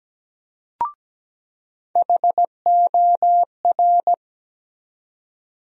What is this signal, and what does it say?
Morse code, heard as a keyed tone, about 17 wpm, 700 Hz. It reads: HOR
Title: USA Call Sign Postfixes - 3 Letters 17wpm